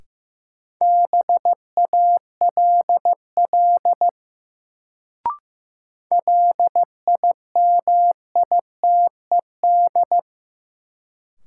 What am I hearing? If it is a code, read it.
Morse code, heard as a keyed tone, about 16 wpm, 700 Hz. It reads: BALL LIMITED
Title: Sentences from Top 900 Words - New Word Drill - 15wpm